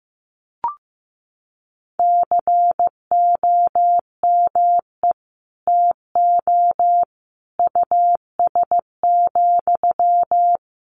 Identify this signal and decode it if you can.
Morse code, heard as a keyed tone, about 15 wpm, 700 Hz. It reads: COME TO US,
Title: Sentences from Top 1300 Words 15wpm